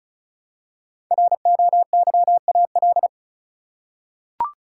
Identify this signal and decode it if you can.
Morse code, heard as a keyed tone, about 36 wpm, 700 Hz. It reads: ROYAL
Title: Five Letter Words - 35wpm